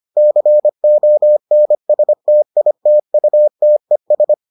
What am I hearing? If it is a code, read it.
Morse code, heard as a keyed tone, about 25 wpm, 600 Hz. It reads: CONSTITUTES